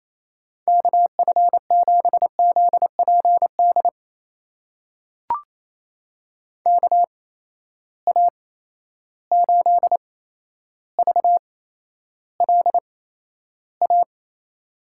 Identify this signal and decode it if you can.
Morse code, heard as a keyed tone, about 28 wpm, spaced as for 12 wpm, 700 Hz. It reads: KF7ZPD KA8VLA
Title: USA Call Signs - 6 Characters - 8x Character Spacing - 28wpm